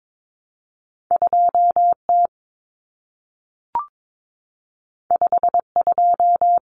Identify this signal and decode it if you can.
Morse code, heard as a keyed tone, about 22 wpm, 700 Hz. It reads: 2T 52
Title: Common Cut Numbers - 2-Digit 22wpm